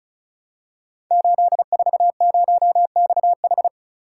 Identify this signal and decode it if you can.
Morse code, heard as a keyed tone, about 36 wpm, 700 Hz. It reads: 840XH
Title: Code Groups - 5 Letter:Numbers - 35wpm